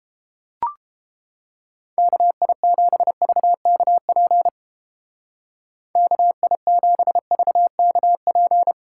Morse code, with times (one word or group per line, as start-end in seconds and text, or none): 2.0-4.5 KI7VKP
6.0-8.7 KI7VKP